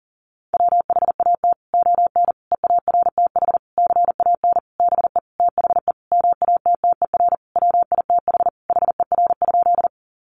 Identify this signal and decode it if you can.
Morse code, heard as a keyed tone, about 40 wpm, 700 Hz. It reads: WHAT ON EARTH CAN BE THE MATTER WITH HER?
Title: Sentences from Top 700 Words 40wpm